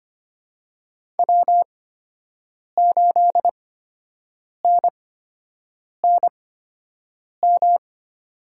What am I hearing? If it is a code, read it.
Morse code, heard as a keyed tone, about 25 wpm, spaced as for 7 wpm, 700 Hz. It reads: W8NNM
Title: USA Call Signs - 5 Characters - 8x Character Spacing - 25wpm